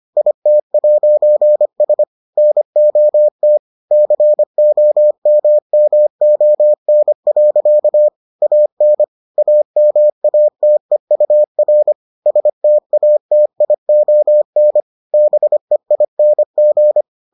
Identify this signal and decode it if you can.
Morse code, heard as a keyed tone, about 25 wpm, 600 Hz. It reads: IT'S NOT COMMON. AN AMATEUR STATION BEING